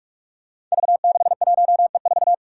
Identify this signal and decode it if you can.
Morse code, heard as a keyed tone, about 46 wpm, 700 Hz. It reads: V61E4
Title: Five Challenging Letter-Number Combinations 45wpm